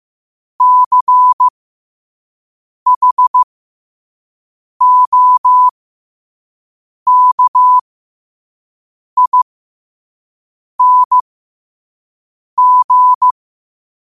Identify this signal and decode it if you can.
Morse code, heard as a keyed tone, about 15 wpm, spaced as for 5 wpm, 1000 Hz. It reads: CHOKING
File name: qotdfiles / 5wpm